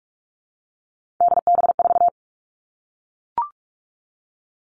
Morse code, heard as a keyed tone, about 44 wpm, 700 Hz. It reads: DB4
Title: All Cut Numbers - 3-Digit 45wpm